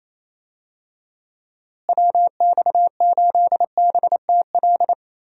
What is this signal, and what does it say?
Morse code, heard as a keyed tone, about 27 wpm, 700 Hz. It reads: WX8BTL
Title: USA Call Signs - 6 Characters - Repeated 3x - 28wpm